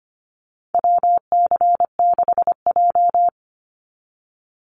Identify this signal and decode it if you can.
Morse code, heard as a keyed tone, about 25 wpm, 700 Hz. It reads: WC6J
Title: NA SOTA Activator Callsigns 25wpm